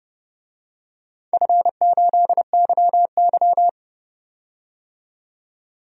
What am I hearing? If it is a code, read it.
Morse code, heard as a keyed tone, about 30 wpm, 700 Hz. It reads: F8YY
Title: DXCC-100 Call Signs - 4 Characters - 30wpm